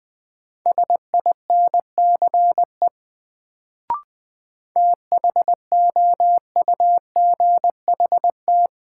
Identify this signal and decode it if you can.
Morse code, heard as a keyed tone, about 20 wpm, 700 Hz. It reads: SINCE THOUGHT
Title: Sets of 2 Words - New Word Drill - 20wpm